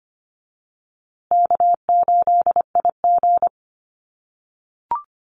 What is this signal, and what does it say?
Morse code, heard as a keyed tone, about 25 wpm, 700 Hz. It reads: K8IG
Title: USA Call Signs Repeated 3x 25wpm